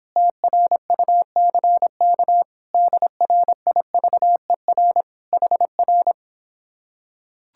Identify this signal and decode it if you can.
Morse code, heard as a keyed tone, about 26 wpm, 700 Hz. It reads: TRUCK DRIVER HR